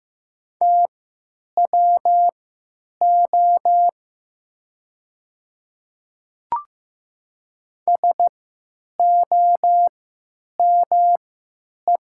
Morse code, heard as a keyed tone, about 15 wpm, spaced as for 9 wpm, 700 Hz. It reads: TWO SOME
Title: 3x Character Spacing - Top 100 Words - 15wpm